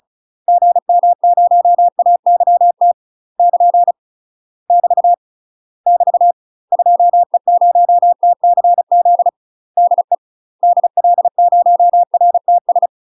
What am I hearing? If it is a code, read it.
Morse code, heard as a keyed tone, about 34 wpm, 700 Hz. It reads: GM0AYT <KN> <BT> <BT> 2E0TCZ DE DL0RTS
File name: UK MASTER QSOs TRAINING / 35WPM / UK MASTER QSOs 0024